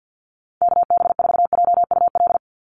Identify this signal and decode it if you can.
Morse code, heard as a keyed tone, about 50 wpm, 700 Hz. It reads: KB4JUL
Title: USA Call Signs 50wpm